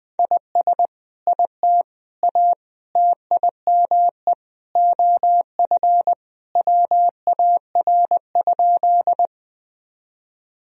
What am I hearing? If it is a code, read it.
Morse code, heard as a keyed tone, about 20 wpm, 700 Hz. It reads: IS IT A TIME OF WAR?
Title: Sentences from Top 1000 Words - Review Part 2 - 20wpm